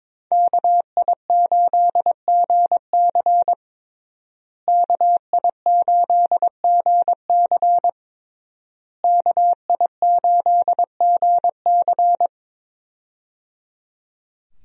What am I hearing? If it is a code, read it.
Morse code, heard as a keyed tone, about 22 wpm, 700 Hz. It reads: KI8GC KI8GC KI8GC
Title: USA Call Signs - 5 Characters - Repeated 3x - 22wpm